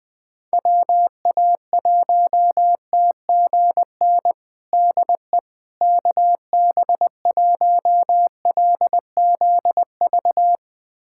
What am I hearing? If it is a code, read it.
Morse code, heard as a keyed tone, about 20 wpm, 700 Hz. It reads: WA1TGN DE KB1LZV
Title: USA Callsign DE Callsign - 6 Characters - 20wpm